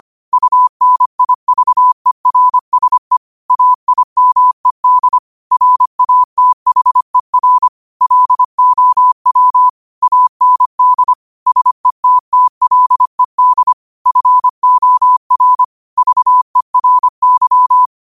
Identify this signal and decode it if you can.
Morse code, heard as a keyed tone, about 25 wpm, 1000 Hz. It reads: UNIVERSE AIMED RATHER LOW AND SETTLED FOR VERY
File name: 25wpm